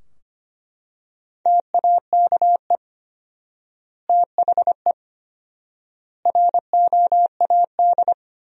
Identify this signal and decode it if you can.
Morse code, heard as a keyed tone, about 25 wpm, 700 Hz. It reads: TAKE THE ROAD
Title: 4x Word-Spacing - Sentences from Top 400 Words - 25wpm